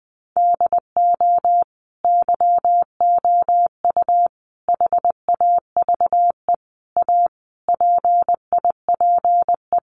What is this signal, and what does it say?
Morse code, heard as a keyed tone, about 20 wpm, 700 Hz. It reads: DO YOU HAVE A PIPE
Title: Sentences from Top 1500 Words 20wpm